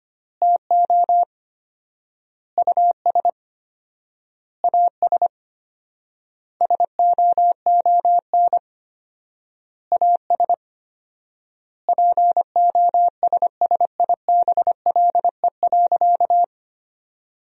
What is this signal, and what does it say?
Morse code, heard as a keyed tone, about 25 wpm, 700 Hz. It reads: TO US AS SOON AS POSSIBLE.
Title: 4x Word-Spacing - Sentences from Top 500 Words - 25wpm